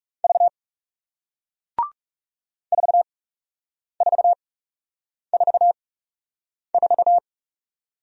Morse code, heard as a keyed tone, about 41 wpm, spaced as for 8 wpm, 700 Hz. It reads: V 4444